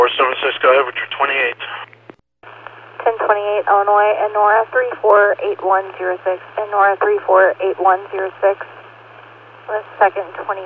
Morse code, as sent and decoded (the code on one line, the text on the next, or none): none